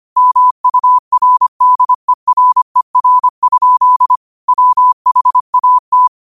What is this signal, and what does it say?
Morse code, heard as a keyed tone, about 25 wpm, 1000 Hz. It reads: MURDERER? WHAT
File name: qotdfiles / 25wpm